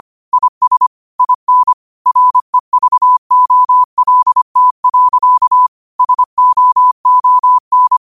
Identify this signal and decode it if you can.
Morse code, heard as a keyed tone, about 25 wpm, 1000 Hz. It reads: IS IN REVOLT. SOON
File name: qotdfiles / 25wpm